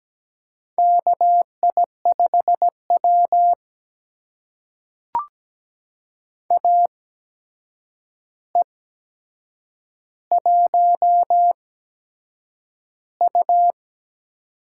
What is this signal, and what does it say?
Morse code, heard as a keyed tone, about 17 wpm, spaced as for 7 wpm, 700 Hz. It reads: KI5W AE1U
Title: USA Call Signs - 4 Characters - 8x Character Spacing - 17wpm